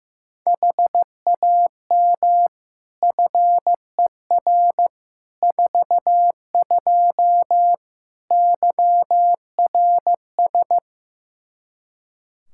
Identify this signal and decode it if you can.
Morse code, heard as a keyed tone, about 15 wpm, 700 Hz. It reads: HAM FER 42 YRS
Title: Practice QSO - Exchange 3 - More Info - 15wpm